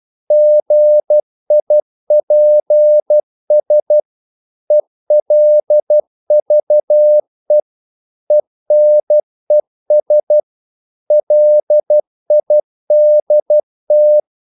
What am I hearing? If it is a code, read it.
Morse code, heard as a keyed tone, about 12 wpm, 600 Hz. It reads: GIPS ELVE ENES LIDT
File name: Korte ord DK 12wpm